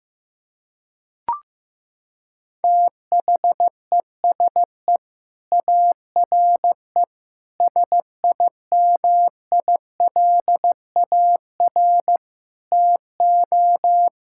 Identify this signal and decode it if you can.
Morse code, heard as a keyed tone, about 16 wpm, 700 Hz. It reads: THESE ARE SIMILAR TO
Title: Sentences from Top 1000 Words - Review Part 5 - 15wpm